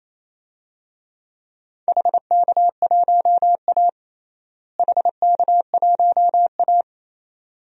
Morse code, heard as a keyed tone, about 28 wpm, 700 Hz. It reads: HK1A HK1A
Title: DXCC-100 Call Signs - 4 Characters - Repeated 2x - 28wpm